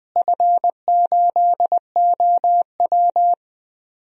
Morse code, heard as a keyed tone, about 20 wpm, 700 Hz. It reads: F8OW
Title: DXCC-100 Call Signs - 4 Characters - Repeated 3x - 20wpm